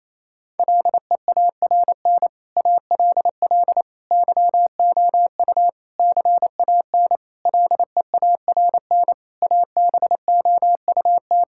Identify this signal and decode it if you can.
Morse code, heard as a keyed tone, about 28 wpm, 700 Hz. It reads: LEARN ALL YOU CAN LEARN ABOUT
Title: Sentences from Top 1000 Words - Review Part 5 - 28wpm